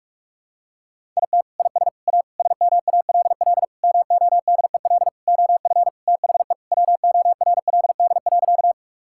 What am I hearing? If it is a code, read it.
Morse code, heard as a keyed tone, about 46 wpm, 700 Hz. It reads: IT IS A SMALL MODEL OF THE WORLD.
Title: Sentences from Top 1000 Words - Review Part 5 - 45wpm